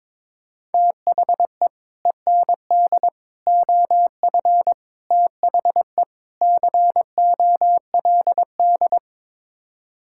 Morse code, heard as a keyed tone, about 21 wpm, 700 Hz. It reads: THE END OF THE COLD
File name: practice / 22wpm / Build A Phrase - 5-Word Phrases - 22wpm